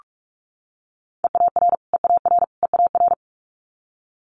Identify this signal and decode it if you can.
Morse code, heard as a keyed tone, about 46 wpm, 700 Hz. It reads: EAR EAR EAR
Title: Multi-letter Phonemes Repeated 3x 45wpm